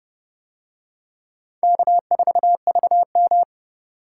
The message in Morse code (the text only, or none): K4VM